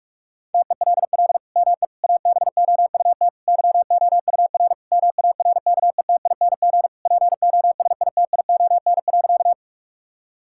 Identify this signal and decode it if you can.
Morse code, heard as a keyed tone, about 46 wpm, 700 Hz. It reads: TELL ME ABOUT YOUR MARKETING POSITION.
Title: Sentences from Top 1000 Words - Review Part 5 - 45wpm